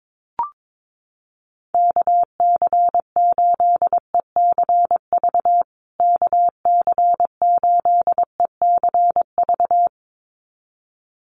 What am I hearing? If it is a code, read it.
Morse code, heard as a keyed tone, about 22 wpm, 700 Hz. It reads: KC8ECV KC8ECV
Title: USA Call Signs Repeated 2x 22wpm